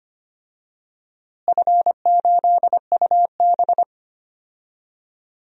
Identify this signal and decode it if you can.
Morse code, heard as a keyed tone, about 25 wpm, 700 Hz. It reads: F8UB
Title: DXCC-100 Call Signs - 4 Characters - Repeated 2x - 25wpm